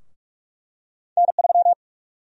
Morse code, heard as a keyed tone, about 46 wpm, 700 Hz. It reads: N3
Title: USA Call Sign Prefixes - 45wpm